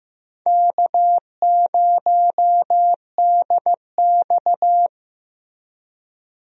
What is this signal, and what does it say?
Morse code, heard as a keyed tone, about 16 wpm, 700 Hz. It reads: K0DX
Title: USA Call Signs - 4 Characters - Repeated 2x - 15wpm